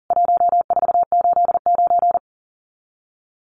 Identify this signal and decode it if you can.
Morse code, heard as a keyed tone, about 40 wpm, 700 Hz. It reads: 1489
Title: CWT Exchange 40wpm